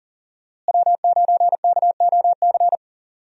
Morse code, heard as a keyed tone, about 38 wpm, 700 Hz. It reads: W9KOC
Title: USA Call Signs - 5 Characters - Repeated 3x - 40wpm